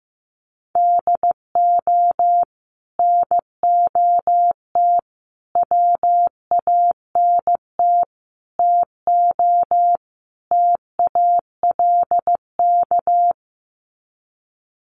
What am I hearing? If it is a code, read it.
Morse code, heard as a keyed tone, about 15 wpm, 700 Hz. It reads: DO NOT WANT TO TALK
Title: Sets of 5 Words 15wpm